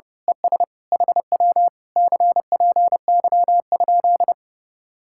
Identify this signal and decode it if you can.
Morse code, heard as a keyed tone, about 29 wpm, 700 Hz. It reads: ES HW CPY?
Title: Speed-Racing Common QSO Phrases - 30wpm